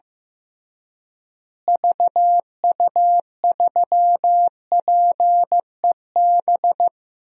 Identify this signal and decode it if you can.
Morse code, heard as a keyed tone, about 14 wpm, 700 Hz. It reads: VU3PEB
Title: DXCC-100 Call Signs - 6 Characters - Repeated 3x - 15wpm